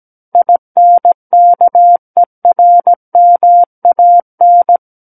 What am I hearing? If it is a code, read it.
Morse code, heard as a keyed tone, about 17 wpm, 700 Hz. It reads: INKERMAN
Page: SKCC Learning Center - Morse Code Practice Files